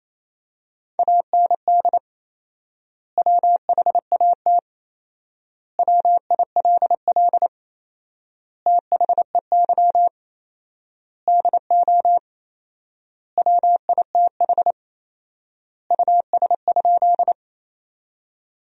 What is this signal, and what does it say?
Morse code, heard as a keyed tone, about 28 wpm, 700 Hz. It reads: AND WHAT WILL THEY DO WITH US?
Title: 4x Word-Spacing - Sentences from Top 200 Words - 28wpm